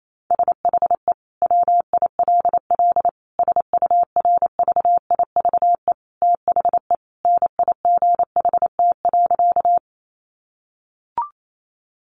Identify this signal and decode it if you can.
Morse code, heard as a keyed tone, about 28 wpm, 700 Hz. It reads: SHE WILL SURVIVE THE NIGHT.
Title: Sentences from Top 1300 Words 28wpm